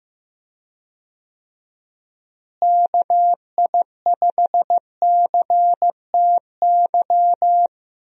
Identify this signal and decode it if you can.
Morse code, heard as a keyed tone, about 15 wpm, 700 Hz. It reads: KI5CTY
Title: USA Call Signs - 6 Characters - Repeated 3x - 15wpm